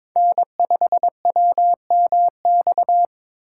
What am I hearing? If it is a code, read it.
Morse code, heard as a keyed tone, about 26 wpm, 700 Hz. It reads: N5WMX
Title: Speed Racing - USA 1x3 Call Signs 22wpm